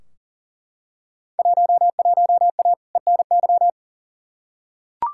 Morse code, heard as a keyed tone, about 39 wpm, 700 Hz. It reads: 11A ENY
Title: Field Day Exchanges Encore - 40wpm